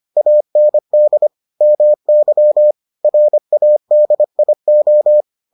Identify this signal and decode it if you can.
Morse code, heard as a keyed tone, about 25 wpm, 600 Hz. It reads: AND MY RADIO